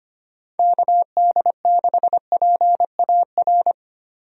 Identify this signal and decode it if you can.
Morse code, heard as a keyed tone, about 25 wpm, 700 Hz. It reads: KD6PAR
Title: USA Call Signs - 6 Characters - Repeated 3x - 25wpm